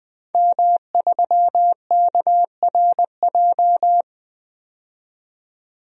Morse code, heard as a keyed tone, about 20 wpm, 700 Hz. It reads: M3KRJ
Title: DXCC-100 Call Signs - 5 Characters - 20wpm